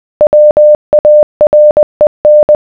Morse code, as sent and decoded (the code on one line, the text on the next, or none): .-- .- .-. -.
WAREN